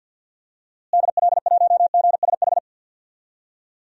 Morse code, heard as a keyed tone, about 51 wpm, 700 Hz. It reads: DL1GSH
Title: DXCC-100 Call Signs - 6 Characters - Repeated 2x - 50wpm